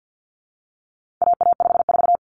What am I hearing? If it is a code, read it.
Morse code, heard as a keyed tone, about 50 wpm, 700 Hz. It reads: AA54
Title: Common Cut Numbers - 4-Digit 50wpm